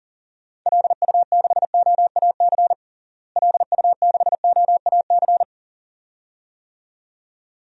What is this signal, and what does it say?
Morse code, heard as a keyed tone, about 37 wpm, 700 Hz. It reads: LU6OAC LU6OAC
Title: DXCC-100 Call Signs - 6 Characters - Repeated 2x - 40wpm